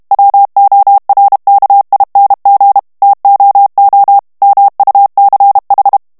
Morse code, heard as a keyed tone, about 32 wpm, 800 Hz. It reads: WORKING TOO MUCH
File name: morse mp3 / ANGLAIS texte clair 1 vitesse 30 mots minute